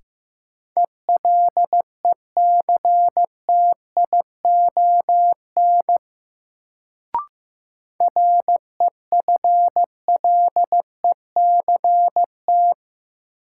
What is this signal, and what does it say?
Morse code, heard as a keyed tone, about 15 wpm, 700 Hz. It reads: ELECTION REFLECT
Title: Sentences from Top 1500 Words - New Word Drill - 15wpm